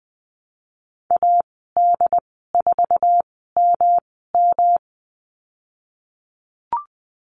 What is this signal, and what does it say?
Morse code, heard as a keyed tone, about 20 wpm, spaced as for 15 wpm, 700 Hz. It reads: AD4MM
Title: 219 Lesson 35g - Number 0 - Call Signs 20wpm